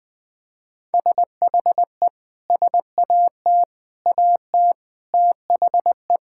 Morse code, heard as a keyed tone, about 20 wpm, 700 Hz. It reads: SHE SAT AT THE
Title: Sentences with 6-Word Phrases - 20wpm